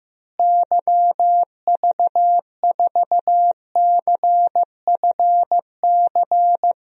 Morse code, heard as a keyed tone, about 15 wpm, 700 Hz. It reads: YV4CFC
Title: DXCC-100 Call Signs - 6 Characters - Repeated 2x - 15wpm